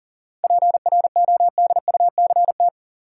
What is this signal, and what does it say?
Morse code, heard as a keyed tone, about 39 wpm, 700 Hz. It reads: PRODUCT
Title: Sentences from Top 1400 Words - New Word Drill Alt - 40wpm